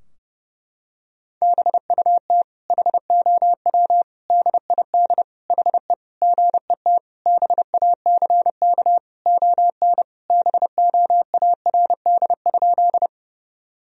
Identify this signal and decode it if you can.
Morse code, heard as a keyed tone, about 30 wpm, 700 Hz. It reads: BUT HOW DID HE GET BACK ON BOARD?
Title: Sentences from Top 1000 Words - Review Part 4 - 30wpm